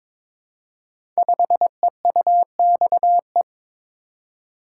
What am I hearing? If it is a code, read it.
Morse code, heard as a keyed tone, about 22 wpm, 700 Hz. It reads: 5EUXE
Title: Code Groups - 5 Letter:Numbers - 22wpm